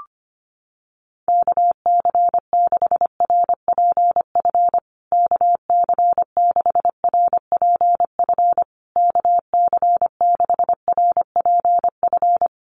Morse code, heard as a keyed tone, about 25 wpm, 700 Hz. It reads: KC6RPF KC6RPF KC6RPF
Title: USA Call Signs Repeated 3x 25wpm